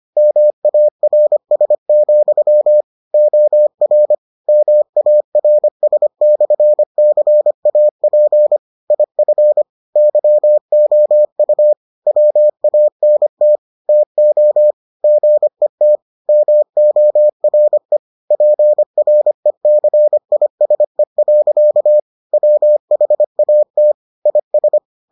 Morse code, heard as a keyed tone, about 25 wpm, 600 Hz. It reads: MARS, OR MARS/CAP IF YOU WANT TO GET MORE PRECISE. WHAT IS